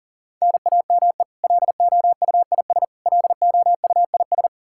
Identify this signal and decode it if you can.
Morse code, heard as a keyed tone, about 37 wpm, 700 Hz. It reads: NAME LOUIS LOUIS
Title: K1USN SST QSO Practice 40wpm